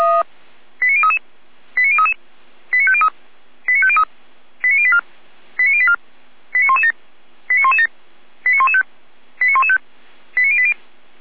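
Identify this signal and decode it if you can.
Five-tone selective call (ZVEI), heard as quick runs of five tones, tones of about 70 ms. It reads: 89033 89033 89503 89503 89084 89084 89117 89117 89115 89115 89080
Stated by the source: ZVEI five-tone selective call received with an SDR.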